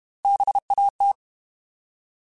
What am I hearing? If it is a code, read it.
Morse code, heard as a keyed tone, about 32 wpm, 800 Hz. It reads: BAT
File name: Alphabetical / Alphabetical mp3s / Bs @30